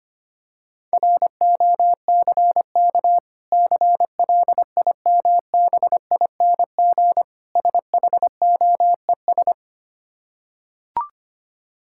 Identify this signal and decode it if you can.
Morse code, heard as a keyed tone, about 25 wpm, 690 Hz. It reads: ROCK CLIMBING SHOES
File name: practice / 25wpm / Difficult Words with Hints - 25wpm